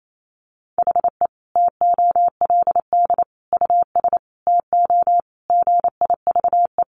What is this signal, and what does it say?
Morse code, heard as a keyed tone, about 28 wpm, 700 Hz. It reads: HE TOLD US TO GIVE
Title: Sentences from Top 1000 Words 28wpm